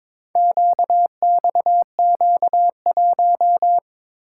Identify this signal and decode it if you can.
Morse code, heard as a keyed tone, about 22 wpm, 700 Hz. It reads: QXQ1
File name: practice / 22wpm / Four Letter-Number Combinations - 22wpm